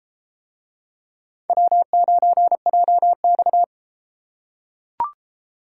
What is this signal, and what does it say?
Morse code, heard as a keyed tone, about 32 wpm, 700 Hz. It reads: W9JX